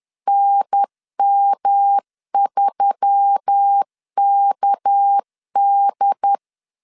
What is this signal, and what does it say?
Morse code, heard as a keyed tone, about 10 wpm, 800 Hz. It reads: NM3KD